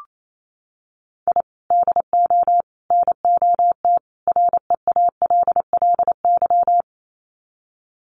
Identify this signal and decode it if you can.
Morse code, heard as a keyed tone, about 28 wpm, 700 Hz. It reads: I DO NOT REALLY
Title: Sets of 4 Words 28wpm